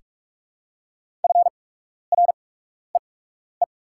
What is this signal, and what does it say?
Morse code, heard as a keyed tone, about 44 wpm, spaced as for 12 wpm, 690 Hz. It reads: FREE
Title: 8x Character Spacing - Top 500 Words - 45wpm